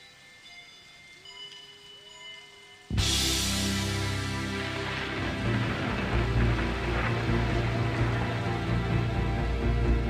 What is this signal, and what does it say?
Dial-up modem handshake.